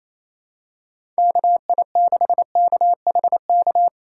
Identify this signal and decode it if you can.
Morse code, heard as a keyed tone, about 28 wpm, 700 Hz. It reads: KI6KHK
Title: USA Call Signs - 6 Characters - 2x Character Spacing - 28wpm